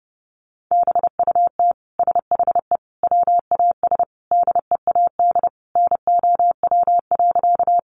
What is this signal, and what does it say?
Morse code, heard as a keyed tone, about 30 wpm, 700 Hz. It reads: BUT SHE WAS DEAD NOW.
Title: Sentences from Top 1500 Words 30wpm